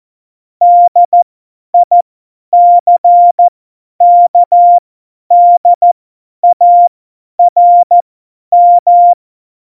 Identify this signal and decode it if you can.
Morse code, heard as a keyed tone, about 14 wpm, spaced as for 10 wpm, 700 Hz. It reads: DICKDARM
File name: Morsekurs / Witze / Witz 25. September 2018, 50 F70